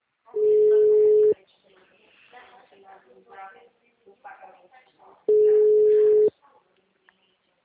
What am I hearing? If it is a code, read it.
Ringback tone (Europe, 425 Hz), with speech.